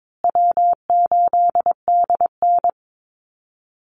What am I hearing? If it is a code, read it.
Morse code, heard as a keyed tone, about 22 wpm, 700 Hz. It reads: W8DN
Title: Call Signs Repeated 2x - 22wpm